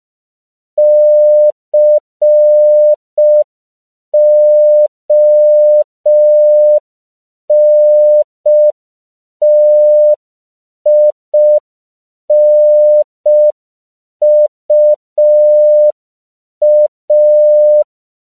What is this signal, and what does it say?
Morse code, heard as a keyed tone, about 5 wpm, 600 Hz. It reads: CONTINUA